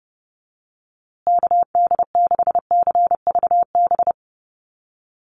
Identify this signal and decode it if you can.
Morse code, heard as a keyed tone, about 30 wpm, 700 Hz. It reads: KD6CVB
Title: USA Call Signs Repeated 3x 30wpm